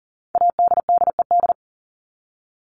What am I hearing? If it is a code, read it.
Morse code, heard as a keyed tone, about 40 wpm, 700 Hz. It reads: ADDED
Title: ED Suffix - Top 100 Words with Base Words 40wpm